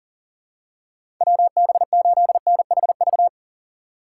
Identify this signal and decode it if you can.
Morse code, heard as a keyed tone, about 40 wpm, 700 Hz. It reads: WB8NHV
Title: USA Call Signs - 6 Characters - 8x Character Spacing - 40wpm